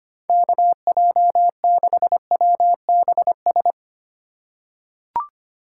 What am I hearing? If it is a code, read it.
Morse code, heard as a keyed tone, about 25 wpm, 700 Hz. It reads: KJ6WBS
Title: USA Call Signs - 6 Characters - 4x Character Spacing - 25wpm